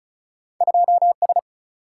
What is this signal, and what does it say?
Morse code, heard as a keyed tone, about 36 wpm, 700 Hz. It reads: 2S
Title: Single-Letter Single-Number 35wpm